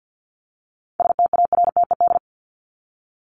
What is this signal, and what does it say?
Morse code, heard as a keyed tone, about 50 wpm, 700 Hz. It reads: STARTED
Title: ED Suffix - Top 100 Words with Base Words 50wpm